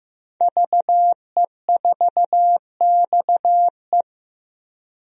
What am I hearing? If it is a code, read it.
Morse code, heard as a keyed tone, about 15 wpm, 700 Hz. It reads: VE4XE
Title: DXCC-100 Call Signs - 5 Characters - Repeated 2x - 15wpm